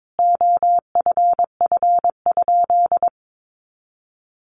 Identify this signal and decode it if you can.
Morse code, heard as a keyed tone, about 22 wpm, 700 Hz. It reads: OFF?
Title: Sentences from Top 500 Words 22wpm